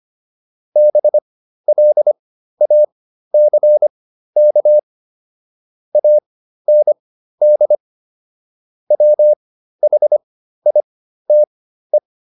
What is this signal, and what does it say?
Morse code, heard as a keyed tone, about 25 wpm, spaced as for 13 wpm, 600 Hz. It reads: BLACK AND WHITE
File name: PR404 13